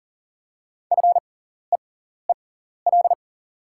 Morse code, heard as a keyed tone, about 38 wpm, spaced as for 14 wpm, 700 Hz. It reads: FEEL
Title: 6x Character Spacing - Top 500 Words - 40wpm